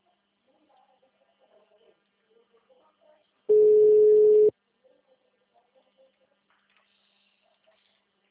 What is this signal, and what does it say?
Ringback tone (Europe, 425 Hz).